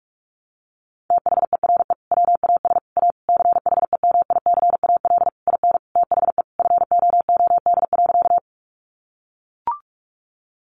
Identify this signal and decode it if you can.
Morse code, heard as a keyed tone, about 44 wpm, 700 Hz. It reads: THERE WAS A CHEMICAL IN THE FOOD.
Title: Sentences from Top 1200 Words 45wpm